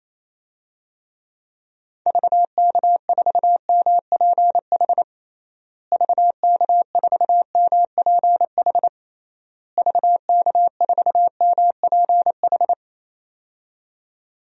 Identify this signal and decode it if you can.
Morse code, heard as a keyed tone, about 28 wpm, 700 Hz. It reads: VK4MPH VK4MPH VK4MPH
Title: DXCC-100 Call Signs - 6 Characters - Repeated 3x - 28wpm